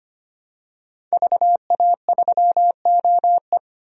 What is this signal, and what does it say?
Morse code, heard as a keyed tone, about 25 wpm, 700 Hz. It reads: VA3OE
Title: DXCC-100 Call Signs - 5 Characters - 25wpm